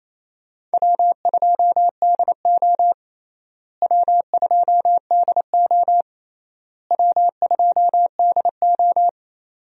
Morse code, heard as a keyed tone, about 28 wpm, 700 Hz. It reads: W2DO W2DO W2DO
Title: USA Call Signs - 4 Characters - Repeated 3x - 28wpm